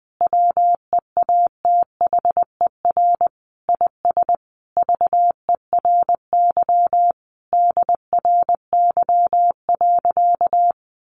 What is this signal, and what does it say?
Morse code, heard as a keyed tone, about 20 wpm, 700 Hz. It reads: WEATHER IS VERY DRY.
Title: Sentences from Top 1400 Words 20wpm